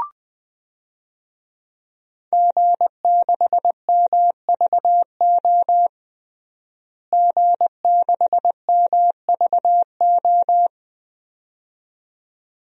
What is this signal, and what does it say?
Morse code, heard as a keyed tone, about 20 wpm, 700 Hz. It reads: G6MVO G6MVO
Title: DXCC-100 Call Signs - 5 Characters - Repeated 2x - 20wpm